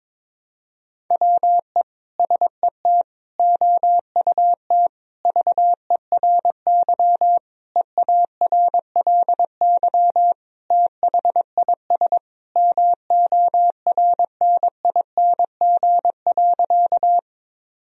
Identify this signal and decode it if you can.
Morse code, heard as a keyed tone, about 22 wpm, 700 Hz. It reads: WE SET OUT VERY EARLY THIS MORNING.
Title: Sentences from Top 1000 Words - Review Part 2 - 22wpm